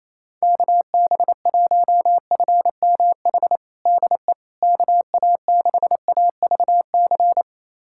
Morse code, heard as a keyed tone, about 28 wpm, 700 Hz. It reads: KB1FMH DE KA6AVC
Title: USA Callsign DE Callsign - 6 Characters - 28wpm